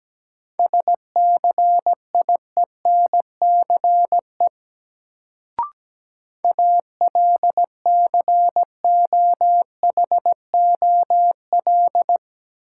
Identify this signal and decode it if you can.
Morse code, heard as a keyed tone, about 17 wpm, 700 Hz. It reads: SCIENCE ALCOHOL
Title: Sentences from Top 1400 Words - New Word Drill Alt - 17wpm